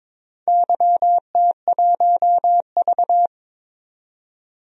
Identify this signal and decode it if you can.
Morse code, heard as a keyed tone, about 22 wpm, 700 Hz. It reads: YT1V
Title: DXCC-100 Call Signs - 4 Characters - 22wpm